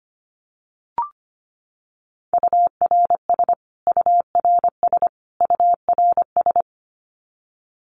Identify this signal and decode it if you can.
Morse code, heard as a keyed tone, about 25 wpm, 700 Hz. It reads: URS URS URS
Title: Abbreviations Repeated 3x - 25wpm